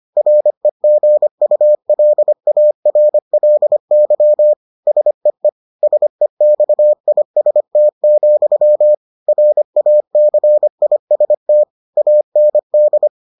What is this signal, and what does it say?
Morse code, heard as a keyed tone, about 25 wpm, 600 Hz. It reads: REGULARLY SEE SEXIST, RACIST AND